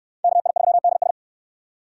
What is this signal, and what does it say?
Morse code, heard as a keyed tone, about 69 wpm, 690 Hz. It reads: KI3DS